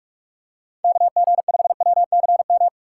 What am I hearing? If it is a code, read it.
Morse code, heard as a keyed tone, about 46 wpm, 700 Hz. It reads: KG5WCM
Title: USA Call Signs - 3x Character Spacing - 45wpm